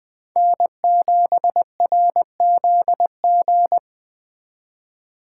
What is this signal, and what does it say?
Morse code, heard as a keyed tone, about 19 wpm, 700 Hz. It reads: N7RZG